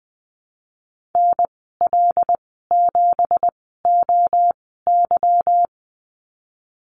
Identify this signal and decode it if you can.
Morse code, heard as a keyed tone, about 20 wpm, spaced as for 14 wpm, 700 Hz. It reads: NL7OY
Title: 167 Lesson 27g - Number 7 - Call Signs 20wpm